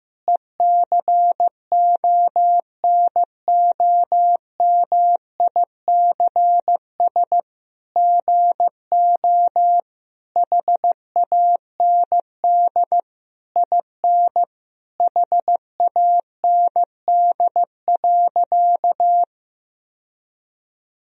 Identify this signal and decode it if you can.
Morse code, heard as a keyed tone, about 16 wpm, 700 Hz. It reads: ECONOMICS GO HAND IN HAND.
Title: Sentences from Top 1000 Words - Review Part 3 - 15wpm